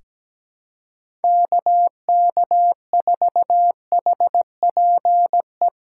Morse code, heard as a keyed tone, about 17 wpm, 700 Hz. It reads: KK4HPE
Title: USA Callsign DE Callsign - 6 Characters - 17wpm